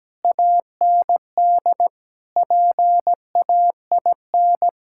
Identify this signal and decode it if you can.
Morse code, heard as a keyed tone, about 17 wpm, 700 Hz. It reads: AND PAIN
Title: Sentences from Top 1600 Words 17wpm